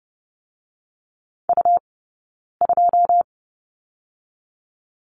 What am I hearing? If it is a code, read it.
Morse code, heard as a keyed tone, about 30 wpm, 700 Hz. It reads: U 2
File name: Lesson 17d - Letter u - Two Characters 30wpm